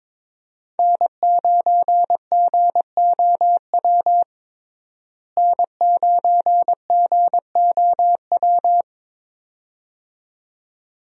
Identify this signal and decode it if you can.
Morse code, heard as a keyed tone, about 22 wpm, 700 Hz. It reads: N9GOW N9GOW
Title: USA Call Signs - 5 Characters - Repeated 2x - 22wpm